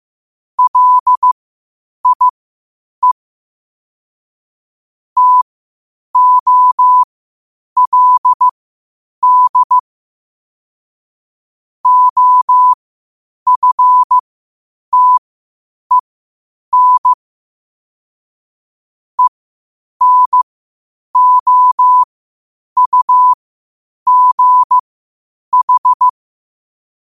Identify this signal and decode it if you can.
Morse code, heard as a keyed tone, about 15 wpm, spaced as for 8 wpm, 1000 Hz. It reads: LIE TOLD OFTEN ENOUGH
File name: qotdfiles / 8wpm